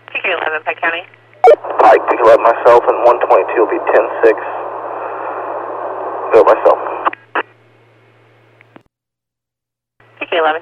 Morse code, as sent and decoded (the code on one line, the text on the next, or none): none